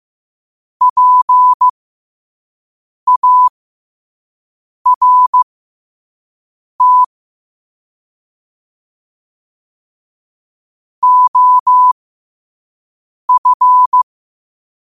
Morse code, heard as a keyed tone, about 15 wpm, spaced as for 5 wpm, 1000 Hz. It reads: PART OF
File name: qotdfiles / 5wpm